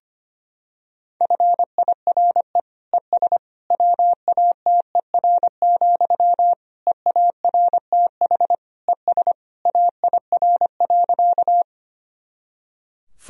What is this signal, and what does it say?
Morse code, heard as a keyed tone, about 25 wpm, 700 Hz. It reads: FIRE ES WATER, EARTH ES AIR.
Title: Sentences from Top 400 Words and Abbreviations 25wpm